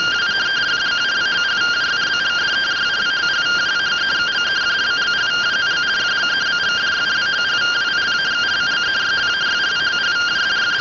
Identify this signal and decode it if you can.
Two-tone teleprinter signal, about 170 Hz shift.